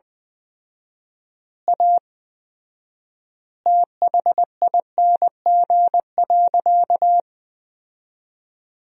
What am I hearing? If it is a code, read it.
Morse code, heard as a keyed tone, about 20 wpm, 700 Hz. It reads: A THING.
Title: 4x Word-Spacing - Sentences from Top 200 Words - 20wpm